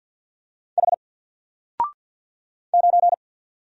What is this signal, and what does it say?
Morse code, heard as a keyed tone, about 48 wpm, 700 Hz. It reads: H 9
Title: Single Letter-Number 50wpm